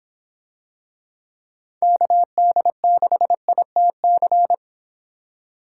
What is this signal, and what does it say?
Morse code, heard as a keyed tone, about 26 wpm, 700 Hz. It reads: KD6ITC